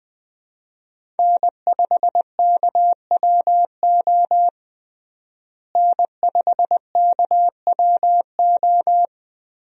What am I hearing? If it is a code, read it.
Morse code, heard as a keyed tone, about 20 wpm, 700 Hz. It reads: N5KWO N5KWO
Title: USA Call Signs - 5 Characters - Repeated 3x - 20wpm